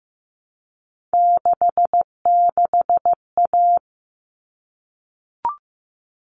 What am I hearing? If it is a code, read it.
Morse code, heard as a keyed tone, about 15 wpm, 700 Hz. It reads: 66A
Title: All Cut Numbers - 3-Digit 15wpm